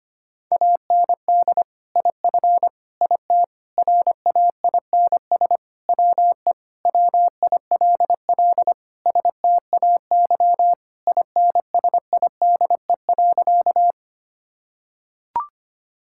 Morse code, heard as a keyed tone, about 25 wpm, 700 Hz. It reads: AND IF IT RAINS WE WILL STAY INSIDE.
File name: practice / 25wpm / Sentences with 2-Word Phrases - 25wpm